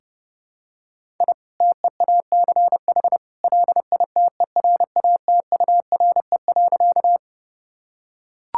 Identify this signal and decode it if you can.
Morse code, heard as a keyed tone, about 30 wpm, 700 Hz. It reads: I TEACH LITERATURE.
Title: Sentences from Top 1000 Words - Review Part 6 - 30wpm